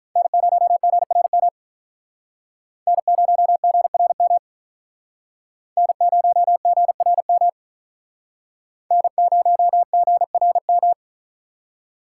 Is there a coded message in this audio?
Yes: Morse code from 0.2 to 10.9 s, heard as a keyed tone, about 46 wpm, 700 Hz. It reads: N0GRM N0GRM N0GRM N0GRM